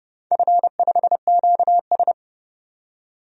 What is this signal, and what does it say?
Morse code, heard as a keyed tone, about 28 wpm, 700 Hz. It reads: F5QS
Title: DXCC-100 Call Signs - 4 Characters - 30wpm